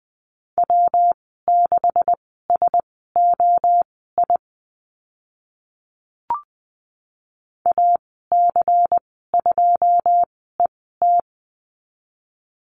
Morse code, heard as a keyed tone, about 20 wpm, spaced as for 14 wpm, 700 Hz. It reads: W6SOI AC2ET
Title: 140 Lesson 23g - Letter f - Call Signs 20wpm